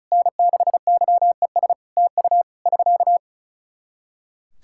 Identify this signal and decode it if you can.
Morse code, heard as a keyed tone, about 36 wpm, 700 Hz. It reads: N6YES TU <SK>
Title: K1USN SST QSO Practice 35wpm